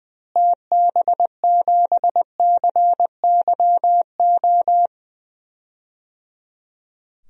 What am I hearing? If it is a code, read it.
Morse code, heard as a keyed tone, about 20 wpm, 700 Hz. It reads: TB7CYO
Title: DXCC-100 Call Signs - 6 Characters - Repeated 3x - 20wpm